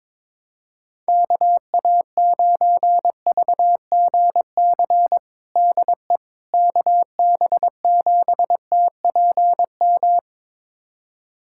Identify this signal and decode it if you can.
Morse code, heard as a keyed tone, about 22 wpm, 700 Hz. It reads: KA9VGC DE KB7TPM
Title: USA Callsign DE Callsign - 6 Characters - 22wpm